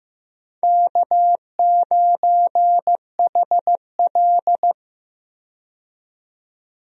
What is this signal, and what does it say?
Morse code, heard as a keyed tone, about 15 wpm, 700 Hz. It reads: K9HL
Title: USA Call Signs - 4 Characters - 15wpm